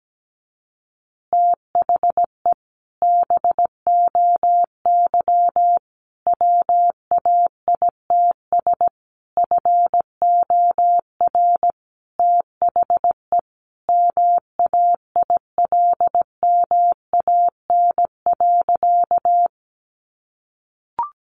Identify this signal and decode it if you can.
Morse code, heard as a keyed tone, about 17 wpm, 700 Hz. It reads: THE BOY WAITS FOR THE MAILMAN.
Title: Additional Easy Phrases 17wpm